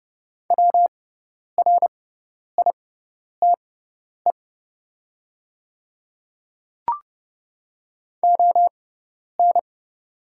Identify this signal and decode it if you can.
Morse code, heard as a keyed tone, about 30 wpm, spaced as for 10 wpm, 700 Hz. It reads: WRITE ON
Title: 6x Character Spacing - Top 300 Words - 30wpm